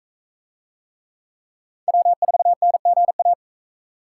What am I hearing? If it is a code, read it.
Morse code, heard as a keyed tone, about 42 wpm, 700 Hz. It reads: W4NGA